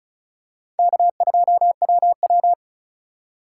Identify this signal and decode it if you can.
Morse code, heard as a keyed tone, about 36 wpm, 700 Hz. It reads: K2WW
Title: USA Call Signs - 4 Characters - 3x Character Spacing - 35wpm